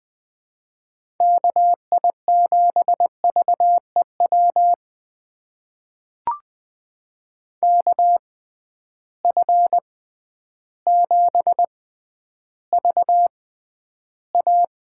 Morse code, heard as a keyed tone, about 20 wpm, spaced as for 11 wpm, 700 Hz. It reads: KI7VEW KF7VA
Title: USA Call Signs - 6 Characters - 6x Character Spacing - 20wpm